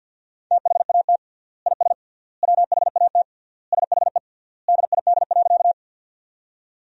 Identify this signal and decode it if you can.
Morse code, heard as a keyed tone, about 50 wpm, 700 Hz. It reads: THAT IS WHAT SHE DID.
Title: 3x Word-Spacing - Sentences from Top 100 Words - 50wpm